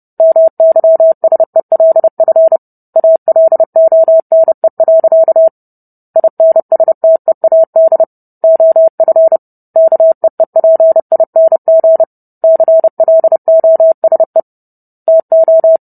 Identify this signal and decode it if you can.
Morse code, heard as a keyed tone, about 30 wpm, 650 Hz. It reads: MYSELF ALONE. INSTEAD OF KEEPING CLOSE TO